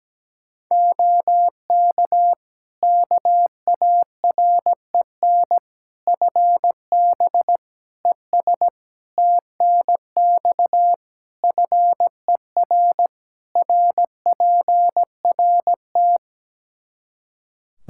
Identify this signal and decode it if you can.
Morse code, heard as a keyed tone, about 17 wpm, 700 Hz. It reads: OK KAREN FB ES TNX FER RPRT
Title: Practice QSO - Exchange 2 - Info - 17wpm